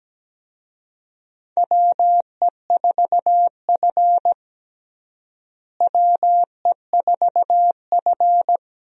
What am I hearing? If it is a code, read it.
Morse code, heard as a keyed tone, about 17 wpm, 700 Hz. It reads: WE4F WE4F
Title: USA Call Signs - 4 Characters - Repeated 3x - 17wpm